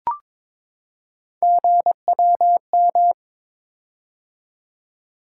Morse code, heard as a keyed tone, about 23 wpm, 700 Hz. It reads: GWM
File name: practice / 22wpm / Three Letter-Number Combinations - 22wpm